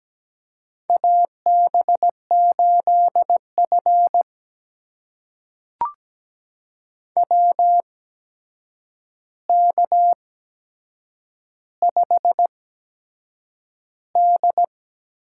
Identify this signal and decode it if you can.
Morse code, heard as a keyed tone, about 17 wpm, spaced as for 7 wpm, 700 Hz. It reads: AB8F WK5D
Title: USA Call Signs - 4 Characters - 8x Character Spacing - 17wpm